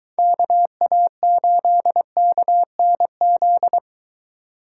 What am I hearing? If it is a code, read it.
Morse code, heard as a keyed tone, about 20 wpm, 700 Hz. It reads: KA8KNZ